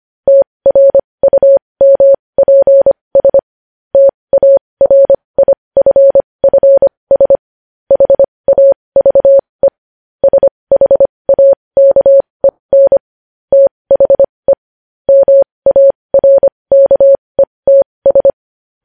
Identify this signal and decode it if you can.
Morse code, heard as a keyed tone, about 25 wpm, spaced as for 20 wpm, 550 Hz. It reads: TRUMPS TARIFFS HAVE SHAKEN THE MARKETS
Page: News headlines in morse code at 20 wpm